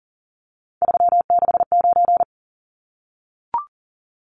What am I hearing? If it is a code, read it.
Morse code, heard as a keyed tone, about 40 wpm, 700 Hz. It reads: 369
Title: Numbers - 3-Digit 40wpm